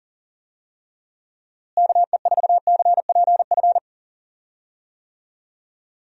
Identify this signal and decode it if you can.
Morse code, heard as a keyed tone, about 40 wpm, 700 Hz. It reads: KE4CPF
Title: USA Call Signs - 6 Characters - Repeated 2x - 40wpm